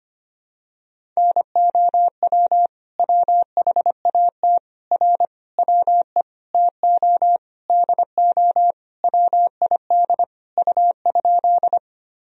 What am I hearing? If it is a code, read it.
Morse code, heard as a keyed tone, about 25 wpm, 700 Hz. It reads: NOW WHAT R WE TO DO WID U?
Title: Sentences from Top 100 Words and Abbreviations 25wpm